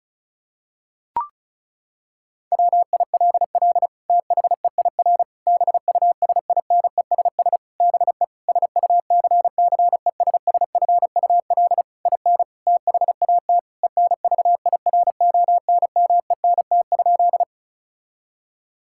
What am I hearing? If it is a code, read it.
Morse code, heard as a keyed tone, about 36 wpm, 700 Hz. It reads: WILL THEIR BUSINESS BE SUCCESSFUL IN THAT ENVIRONMENT?
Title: Sentences from Top 1000 Words - Review Part 5 - 35wpm